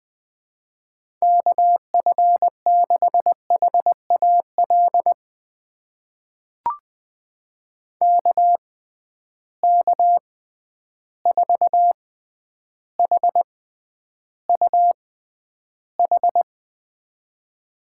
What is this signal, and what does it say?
Morse code, heard as a keyed tone, about 20 wpm, spaced as for 10 wpm, 700 Hz. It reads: KF6HAL KK4HUH
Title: USA Call Signs - 6x Character Spacing - 20wpm